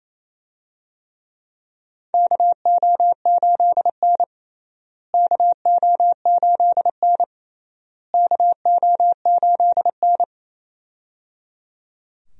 Morse code, heard as a keyed tone, about 27 wpm, 700 Hz. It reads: KO8N KO8N KO8N
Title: USA Call Signs - 4 Characters - Repeated 3x - 28wpm